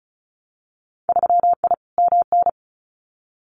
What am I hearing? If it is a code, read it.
Morse code, heard as a keyed tone, about 34 wpm, 700 Hz. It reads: 3I MN
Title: Winter Field Day Exchanges 35wpm